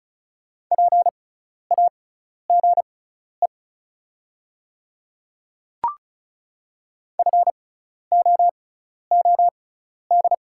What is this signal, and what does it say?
Morse code, heard as a keyed tone, about 36 wpm, spaced as for 12 wpm, 690 Hz. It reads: PAGE FOOD
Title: 6x Character Spacing - Top 300 Words - 35wpm